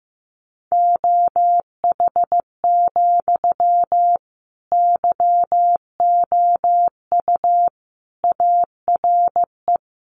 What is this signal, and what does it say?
Morse code, heard as a keyed tone, about 15 wpm, 700 Hz. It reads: OH, YOU ARE
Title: Sentences from Top 900 Words 15wpm